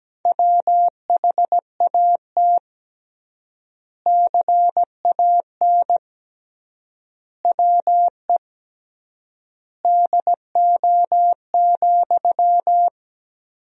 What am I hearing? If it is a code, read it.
Morse code, heard as a keyed tone, about 17 wpm, 700 Hz. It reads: WHAT CAN WE DO,
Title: 3x Word-Spacing - Sentences from Top 100 Words - 17wpm